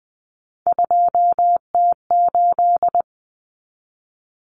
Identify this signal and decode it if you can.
Morse code, heard as a keyed tone, about 20 wpm, 700 Hz. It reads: 2T8
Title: Common Cut Numbers - 3-Digit 20wpm